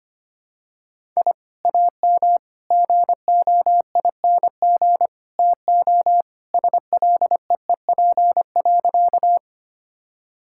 Morse code, heard as a keyed tone, about 25 wpm, 700 Hz. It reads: I AM GOING TO SLEEP.
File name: practice / 25wpm / Sentences with 2-Word Phrases - 25wpm